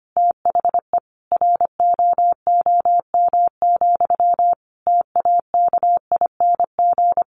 Morse code, heard as a keyed tone, about 25 wpm, 700 Hz. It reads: THE ROOM, TAKING
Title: A Sound Of Thunder 25wpm